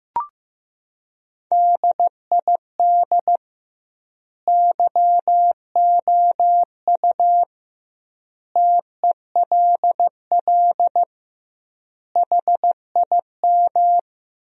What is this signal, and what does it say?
Morse code, heard as a keyed tone, about 15 wpm, 700 Hz. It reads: DID YOU TELL HIM
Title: 2x Word-Spacing - Sentences from Top 400 Words - 15wpm